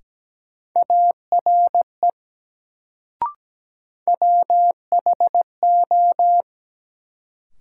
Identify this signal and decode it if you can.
Morse code, heard as a keyed tone, about 17 wpm, 700 Hz. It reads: ARE WHO
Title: Sentences from Top 100 Words - New Word Drill - 17wpm